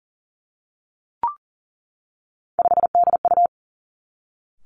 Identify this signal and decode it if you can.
Morse code, heard as a keyed tone, about 38 wpm, 700 Hz. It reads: 5DU
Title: Three Challenging Letter-Number Combinations 40wpm